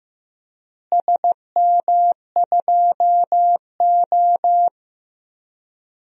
Morse code, heard as a keyed tone, about 15 wpm, 700 Hz. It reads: SM2O
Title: DXCC-100 Call Signs - 4 Characters - Repeated 2x - 15wpm